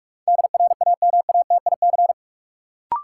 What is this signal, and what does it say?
Morse code, heard as a keyed tone, about 46 wpm, 700 Hz. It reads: DRAMATIC
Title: Sentences from Top 1300 Words - New Word Drill Alt - 45wpm